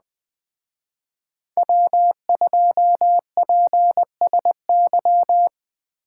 Morse code, heard as a keyed tone, about 21 wpm, 700 Hz. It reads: W2PSY
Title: Speed Racing - USA 1x3 Call Signs 17wpm